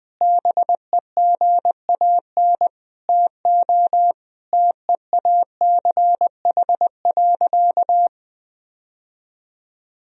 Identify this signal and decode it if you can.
Morse code, heard as a keyed tone, about 20 wpm, 700 Hz. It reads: BEGAN TO TEACH.
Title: Sentences from Top 1000 Words - Review Part 2 - 20wpm